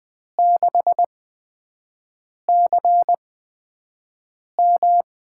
Morse code, heard as a keyed tone, about 20 wpm, spaced as for 5 wpm, 700 Hz. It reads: 6CM
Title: USA Call Signs - 8x Character Spacing - 20wpm